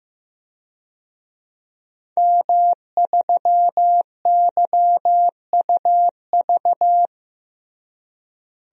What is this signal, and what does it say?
Morse code, heard as a keyed tone, about 15 wpm, 700 Hz. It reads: M3YUV
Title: DXCC-100 Call Signs - 5 Characters - Repeated 2x - 15wpm